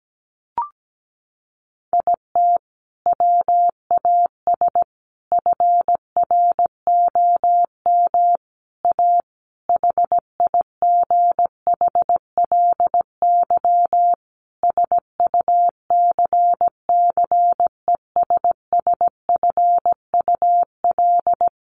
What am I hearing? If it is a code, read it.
Morse code, heard as a keyed tone, about 17 wpm, 700 Hz. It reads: IT WAS FROM A HIGHLY SUCCESSFUL
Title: Sentences from Top 1100 Words 17wpm